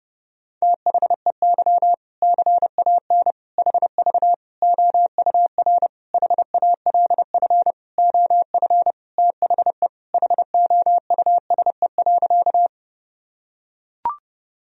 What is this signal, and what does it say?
Morse code, heard as a keyed tone, about 30 wpm, 700 Hz. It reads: THEY CAN HV OUR HALF OF THE HOUSE.
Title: Sentences from Top 400 Words and Abbreviations 30wpm